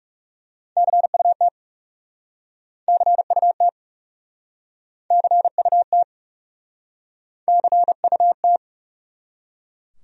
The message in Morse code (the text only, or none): CUT CUT CUT CUT